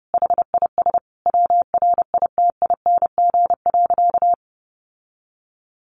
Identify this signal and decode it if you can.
Morse code, heard as a keyed tone, about 30 wpm, 700 Hz. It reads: HIS WRITING.
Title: Sentences from Top 800 Words 30wpm